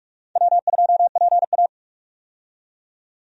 Morse code, heard as a keyed tone, about 46 wpm, 700 Hz. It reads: W2PA
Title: USA Call Signs - 4 Characters - 45wpm